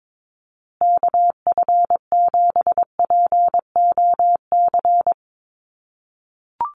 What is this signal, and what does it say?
Morse code, heard as a keyed tone, about 22 wpm, 700 Hz. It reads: KF7POC
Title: USA Call Signs Repeated 2x 22wpm